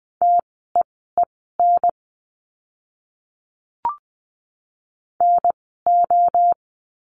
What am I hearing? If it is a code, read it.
Morse code, heard as a keyed tone, about 20 wpm, spaced as for 14 wpm, 700 Hz. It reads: TEEN NO
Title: 024 Lesson 5f - Letter o - Words 20wpm